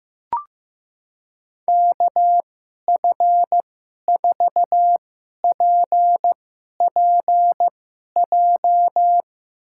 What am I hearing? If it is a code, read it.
Morse code, heard as a keyed tone, about 15 wpm, spaced as for 11 wpm, 700 Hz. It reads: KF4PPJ
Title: USA Call Signs - 6 Characters - 2x Character Spacing - 15wpm